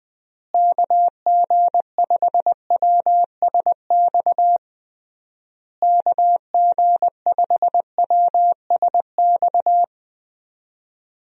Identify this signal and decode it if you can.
Morse code, heard as a keyed tone, about 20 wpm, 700 Hz. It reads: KG5WSX KG5WSX
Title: USA Call Signs - 6 Characters - Repeated 3x - 20wpm